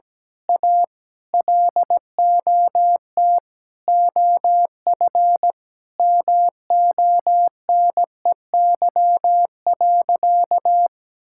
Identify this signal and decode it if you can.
Morse code, heard as a keyed tone, about 17 wpm, 700 Hz. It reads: A LOT OF MONEY.
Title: Sentences from Top 1000 Words - Review Part 6 - 17wpm